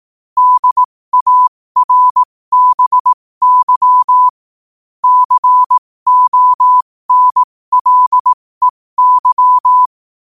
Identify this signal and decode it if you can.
Morse code, heard as a keyed tone, about 18 wpm, spaced as for 15 wpm, 1000 Hz. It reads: DARBY CONLEY
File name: qotdfiles / 15wpm